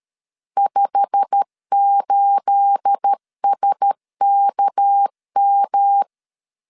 Morse code, heard as a keyed tone, about 13 wpm, 800 Hz. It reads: 58SKM
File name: CW / DET / 1 9 1500C S B 12wpm 3819